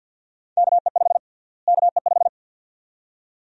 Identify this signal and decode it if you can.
Morse code, heard as a keyed tone, about 47 wpm, 700 Hz. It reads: KE5 KE5
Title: USA Call Sign Prefixes - Repeated 2x - 50wpm